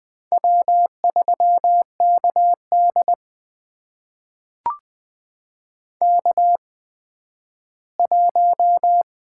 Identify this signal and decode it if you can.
Morse code, heard as a keyed tone, about 20 wpm, spaced as for 12 wpm, 700 Hz. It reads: W3KD K1
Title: USA Call Signs - 4 Characters - 8x Character Spacing - 20wpm